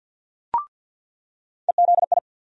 Morse code, heard as a keyed tone, about 47 wpm, 700 Hz. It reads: EZI
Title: USA Call Sign Postfixes - 3 Letters 50wpm